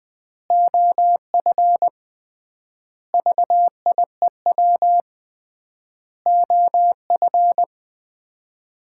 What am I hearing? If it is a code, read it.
Morse code, heard as a keyed tone, about 21 wpm, 700 Hz. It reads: OF VIEW OF
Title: 3x Word-Spacing - Sets of 5 Words - 20wpm